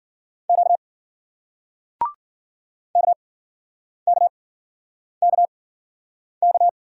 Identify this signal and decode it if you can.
Morse code, heard as a keyed tone, about 53 wpm, spaced as for 9 wpm, 700 Hz. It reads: Q KKKK